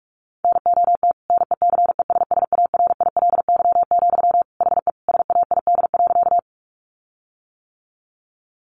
Morse code, heard as a keyed tone, about 44 wpm, 700 Hz. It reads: NOT NECESSARILY, HE SAID.
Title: Sentences from Top 1400 Words 45wpm